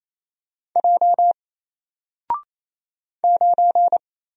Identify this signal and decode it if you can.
Morse code, heard as a keyed tone, about 28 wpm, 700 Hz. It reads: J 9
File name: Single-Letter-Number-28wpm